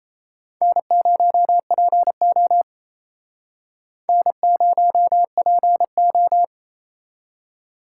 Morse code, heard as a keyed tone, about 39 wpm, 700 Hz. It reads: N0PO N0PO